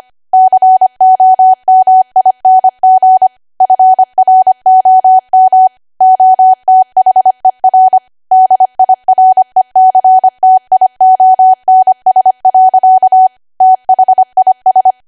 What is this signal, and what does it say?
Morse code, heard as a keyed tone, about 25 wpm, 750 Hz. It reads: COMING FROM OTHER DIRECTIONS. THIS
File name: Archive / 25 WPM / 25WPM